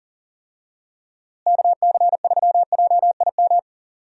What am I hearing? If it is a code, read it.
Morse code, heard as a keyed tone, about 40 wpm, 700 Hz. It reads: KC3JIM
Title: USA Call Signs - 6 Characters - Repeated 3x - 40wpm